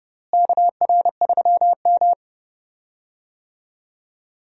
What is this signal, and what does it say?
Morse code, heard as a keyed tone, about 30 wpm, 700 Hz. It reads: KR3M
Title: USA Call Signs - 4 Characters - Repeated 3x - 30wpm